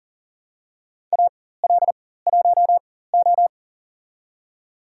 Morse code, heard as a keyed tone, about 40 wpm, spaced as for 19 wpm, 700 Hz. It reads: AL1O
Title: USA Call Signs - 4 Characters - 4x Character Spacing - 40wpm